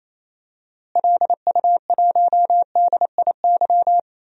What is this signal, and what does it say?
Morse code, heard as a keyed tone, about 28 wpm, 700 Hz. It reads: LU1DIY
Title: DXCC-100 Call Signs - 6 Characters - Repeated 3x - 28wpm